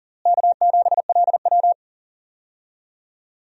Morse code, heard as a keyed tone, about 40 wpm, 700 Hz. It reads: K7LW
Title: USA Call Signs - 4 Characters - Repeated 3x - 40wpm